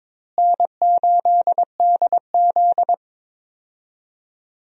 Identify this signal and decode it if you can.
Morse code, heard as a keyed tone, about 22 wpm, 700 Hz. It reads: N8DZ
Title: USA Call Signs - 4 Characters - Repeated 2x - 22wpm